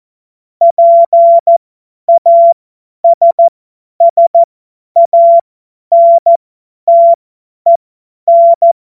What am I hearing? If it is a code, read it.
Morse code, heard as a keyed tone, about 14 wpm, spaced as for 10 wpm, 680 Hz. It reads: PASSANTEN
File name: Witz-23-September-2025-50-F70